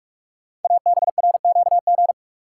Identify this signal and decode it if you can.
Morse code, heard as a keyed tone, about 44 wpm, 690 Hz. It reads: ABRQG
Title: Code Groups - 5 Letters - 45wpm